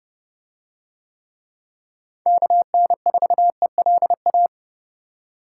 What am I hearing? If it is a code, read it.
Morse code, heard as a keyed tone, about 28 wpm, 700 Hz. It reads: KN4ELA